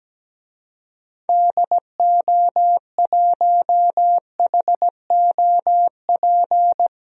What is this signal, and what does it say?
Morse code, heard as a keyed tone, about 17 wpm, 700 Hz. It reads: DO1HOP
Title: DXCC-100 Call Signs - 6 Characters - 17wpm